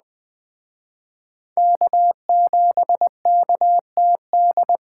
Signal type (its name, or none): Morse code